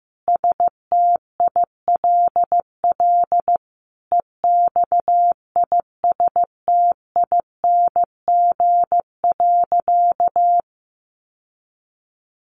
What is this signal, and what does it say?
Morse code, heard as a keyed tone, about 15 wpm, 700 Hz. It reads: STILL EXISTING.
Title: Sentences from Top 1500 Words 15wpm